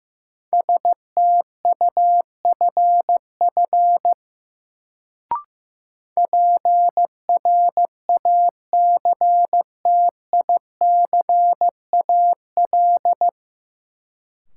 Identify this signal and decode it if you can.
Morse code, heard as a keyed tone, about 16 wpm, 700 Hz. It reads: STUFF PRACTICAL
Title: Sentences from Top 1400 Words - New Word Drill - 15wpm